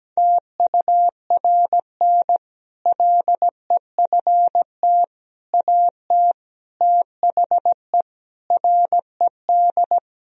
Morse code, heard as a keyed tone, about 17 wpm, 700 Hz. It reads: TURN LEFT AT THE RED
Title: Sentences from Top 1000 Words - Review Part 2 - 17wpm